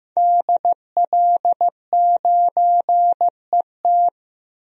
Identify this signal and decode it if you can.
Morse code, heard as a keyed tone, about 15 wpm, 700 Hz. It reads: DL9ET
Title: DXCC-100 Call Signs - 5 Characters - Repeated 3x - 15wpm